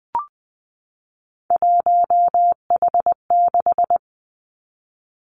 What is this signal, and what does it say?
Morse code, heard as a keyed tone, about 20 wpm, 700 Hz. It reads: 1H6
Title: Three Challenging Letter-Number Combinations 20wpm